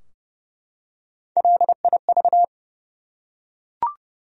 Morse code, heard as a keyed tone, about 30 wpm, 700 Hz. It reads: LIV
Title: USA Common Names - Max 3 Letters - 30wpm